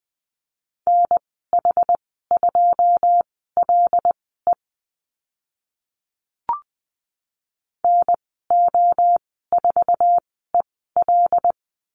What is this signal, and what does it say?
Morse code, heard as a keyed tone, about 20 wpm, spaced as for 15 wpm, 700 Hz. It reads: NH2LE NO4EL
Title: 082 Lesson 14g - Number 2 - Call Signs 20wpm